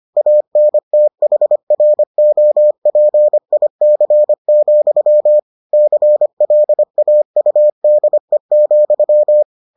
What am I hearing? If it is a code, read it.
Morse code, heard as a keyed tone, about 25 wpm, 600 Hz. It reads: ANTHROPIC, CLAUDE,